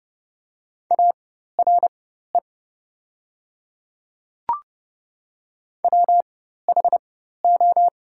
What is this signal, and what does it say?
Morse code, heard as a keyed tone, about 30 wpm, spaced as for 14 wpm, 690 Hz. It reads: ARE WHO
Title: 4x Character Spacing - Top 300 Words - 30wpm